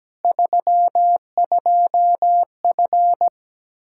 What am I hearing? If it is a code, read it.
Morse code, heard as a keyed tone, about 17 wpm, 700 Hz. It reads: 32F
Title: Practice QSO - Exchange 2 - Info - 17wpm